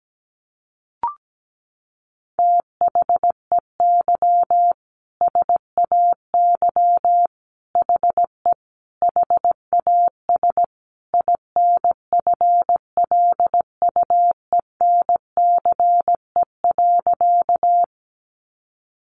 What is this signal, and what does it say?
Morse code, heard as a keyed tone, about 17 wpm, 700 Hz. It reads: THEY SAY HE HAS INFLUENCE.
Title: Sentences from Top 1100 Words 17wpm